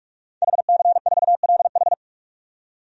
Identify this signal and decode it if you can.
Morse code, heard as a keyed tone, about 46 wpm, 700 Hz. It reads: HC4LH
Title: DXCC-100 Call Signs - 5 Characters - 45wpm